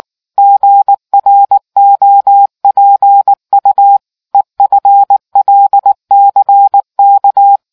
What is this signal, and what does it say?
Morse code, heard as a keyed tone, about 19 wpm, 800 Hz. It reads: GROPU EFLCK